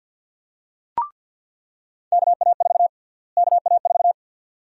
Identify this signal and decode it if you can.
Morse code, heard as a keyed tone, about 49 wpm, 700 Hz. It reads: KA4 KA4
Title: USA Call Sign Prefixes - Repeated 2x - 50wpm